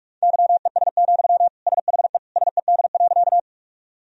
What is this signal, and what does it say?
Morse code, heard as a keyed tone, about 46 wpm, 700 Hz. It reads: YES, SHE SED.
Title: Sentences from Top 500 Words and Abbreviations 45wpm